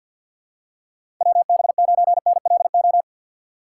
Morse code, heard as a keyed tone, about 47 wpm, 700 Hz. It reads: WB9NLO
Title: USA Call Signs - 6 Characters - 8x Character Spacing - 50wpm